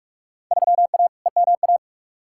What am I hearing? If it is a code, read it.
Morse code, heard as a keyed tone, about 46 wpm, 700 Hz. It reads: 3A EMA
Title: Field Day Exchanges - 45wpm